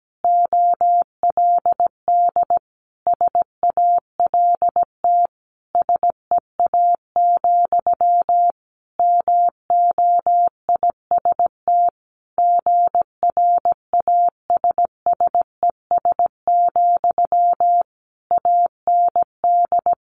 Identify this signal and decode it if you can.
Morse code, heard as a keyed tone, about 17 wpm, 700 Hz. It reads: OLD SALT SEA, MOIST GRASSES, AND
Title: A Sound Of Thunder 17wpm